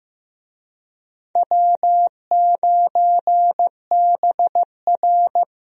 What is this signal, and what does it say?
Morse code, heard as a keyed tone, about 15 wpm, 700 Hz. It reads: W9BR
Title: USA Call Signs - 4 Characters - Repeated 3x - 15wpm